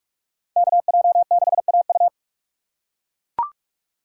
Morse code, heard as a keyed tone, about 41 wpm, 700 Hz. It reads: KJ6AU